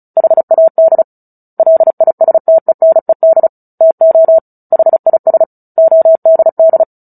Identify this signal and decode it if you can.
Morse code, heard as a keyed tone, about 34 wpm, 650 Hz. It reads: HAD LISTENED TO HIS ODD
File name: Book Two / 09-Chapter 9 - 35 WPM